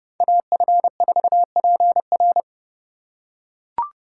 Morse code, heard as a keyed tone, about 28 wpm, 700 Hz. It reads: AF4PR